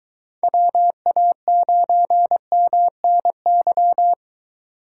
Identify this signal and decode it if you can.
Morse code, heard as a keyed tone, about 22 wpm, 700 Hz. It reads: WA9MNY